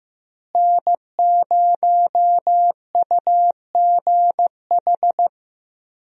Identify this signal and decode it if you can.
Morse code, heard as a keyed tone, about 15 wpm, 700 Hz. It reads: N0UGH
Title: USA Call Signs - 5 Characters - Repeated 3x - 15wpm